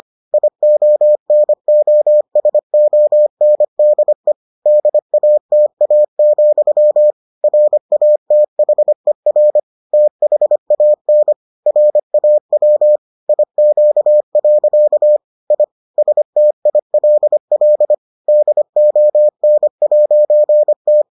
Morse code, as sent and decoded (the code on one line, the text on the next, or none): .. --- -. --- ... --- -. -.. / -.. .- - .- --..-- / .-. .- - .... .-. / - .... .- -. / .-. .- .-- / .. --.- .-.-.- / .. / ... - .. .-.. .-.. / -.. --- -. .----. -
IONOSONDE DATA, RATHER THAN RAW IQ. I STILL DON'T